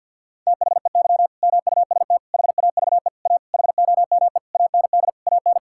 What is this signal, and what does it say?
Morse code, heard as a keyed tone, about 50 wpm, 700 Hz. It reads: THEY MUST HAVE A HOME AND AN
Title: Sentences from Top 1000 Words - Review Part 5 - 50wpm